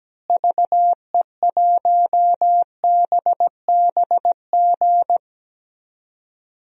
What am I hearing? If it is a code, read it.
Morse code, heard as a keyed tone, about 17 wpm, 700 Hz. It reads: VE1BBG
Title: DXCC-100 Call Signs - 6 Characters - Repeated 3x - 17wpm